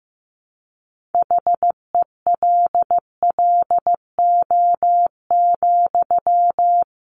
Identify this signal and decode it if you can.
Morse code, heard as a keyed tone, about 15 wpm, 700 Hz. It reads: HELLO,
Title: Sentences from Top 1100 Words 15wpm